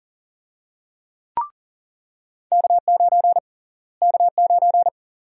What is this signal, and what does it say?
Morse code, heard as a keyed tone, about 40 wpm, 700 Hz. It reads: K9 K9
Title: USA Call Sign Prefixes - Repeated 2x - 40wpm